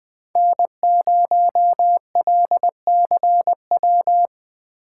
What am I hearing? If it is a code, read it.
Morse code, heard as a keyed tone, about 20 wpm, 700 Hz. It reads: N0LCW